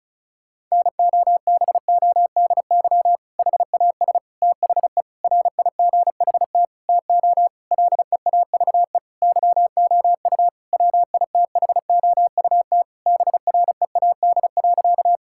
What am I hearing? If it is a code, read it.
Morse code, heard as a keyed tone, about 36 wpm, 700 Hz. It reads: NOBODY HAS THE RIGHT TO LEAVE YOU WITHOUT BREAD.
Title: Sentences from Top 1600 Words 35wpm